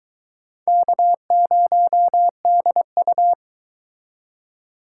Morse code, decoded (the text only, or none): K0DU